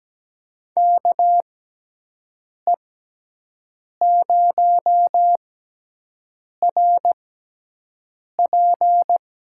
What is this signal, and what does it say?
Morse code, heard as a keyed tone, about 17 wpm, spaced as for 6 wpm, 700 Hz. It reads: KE0RP
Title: USA Call Signs - 6 Characters - 6x Character Spacing - 17wpm